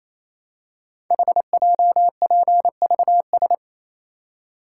Morse code, heard as a keyed tone, about 27 wpm, 700 Hz. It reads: HJPVS
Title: Code Groups - 5 Letters - 28wpm